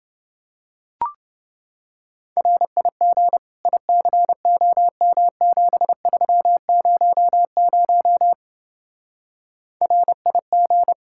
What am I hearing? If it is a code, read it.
Morse code, heard as a keyed tone, about 27 wpm, 700 Hz. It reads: RIG ICOM7300 RIG